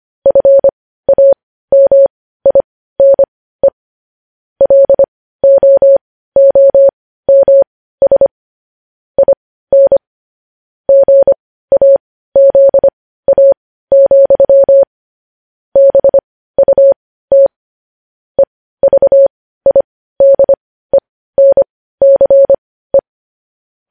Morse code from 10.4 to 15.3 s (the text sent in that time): GAZA,